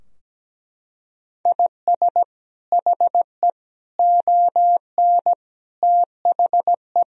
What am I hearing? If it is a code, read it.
Morse code, heard as a keyed tone, about 17 wpm, 700 Hz. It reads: IS HE ON THE
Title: Sentences from Top 1000 Words - Review Part 2 - 17wpm